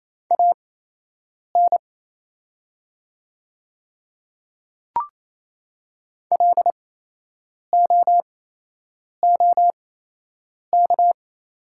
Morse code, heard as a keyed tone, about 28 wpm, spaced as for 8 wpm, 700 Hz. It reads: AN LOOK
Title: 8x Character Spacing - Top 100 Words - 28wpm